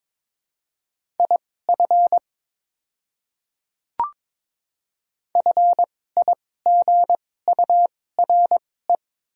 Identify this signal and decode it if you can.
Morse code, heard as a keyed tone, about 21 wpm, spaced as for 16 wpm, 690 Hz. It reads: IF FIGURE
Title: 2x Character Spacing - Top 500 Words - 22wpm